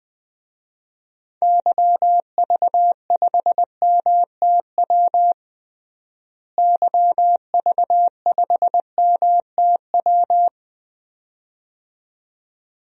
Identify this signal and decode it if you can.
Morse code, heard as a keyed tone, about 20 wpm, 700 Hz. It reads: YV5MTW YV5MTW
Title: DXCC-100 Call Signs - 6 Characters - Repeated 2x - 20wpm